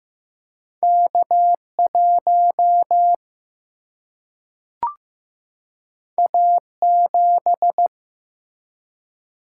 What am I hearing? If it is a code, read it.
Morse code, heard as a keyed tone, about 15 wpm, 700 Hz. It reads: K 1 A 7
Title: Single-Letter Single-Number 15wpm